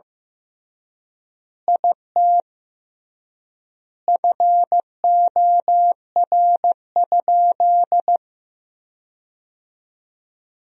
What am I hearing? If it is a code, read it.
Morse code, heard as a keyed tone, about 15 wpm, 700 Hz. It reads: IT FOR?
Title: 3x Word-Spacing - Sentences from Top 100 Words - 15wpm